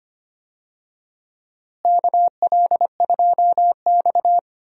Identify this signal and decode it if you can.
Morse code, heard as a keyed tone, about 25 wpm, 700 Hz. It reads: KL2X
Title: USA Call Signs - 4 Characters - Repeated 3x - 25wpm